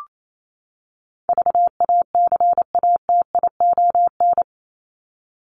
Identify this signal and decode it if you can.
Morse code, heard as a keyed tone, about 28 wpm, 700 Hz. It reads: VACATION
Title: Top 1501-1750 Words 28wpm